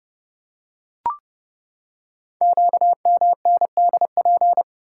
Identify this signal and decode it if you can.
Morse code, heard as a keyed tone, about 29 wpm, 700 Hz. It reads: QMNDP
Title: Code Groups - 5 Letters - 30wpm